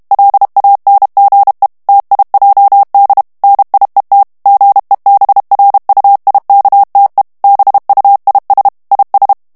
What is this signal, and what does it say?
Morse code, heard as a keyed tone, about 32 wpm, 800 Hz. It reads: LANGE TIJD NIET GEBRUIKTE BUIS IS
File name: morse mp3 / HOLLANDAIS texte clair 2 vitesse 30 mots minute